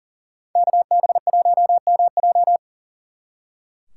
Morse code, heard as a keyed tone, about 40 wpm, 700 Hz. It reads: KB1MJ
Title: Speed Racing - USA 2x2 Call Signs 40wpm